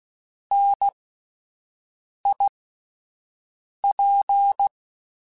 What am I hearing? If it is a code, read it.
Morse code, heard as a keyed tone, about 16 wpm, spaced as for 5 wpm, 800 Hz. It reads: NIP
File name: Alphabetical / Alphabetical mp3s / Ns @5